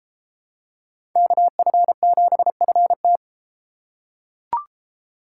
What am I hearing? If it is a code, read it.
Morse code, heard as a keyed tone, about 33 wpm, 700 Hz. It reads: KF7FT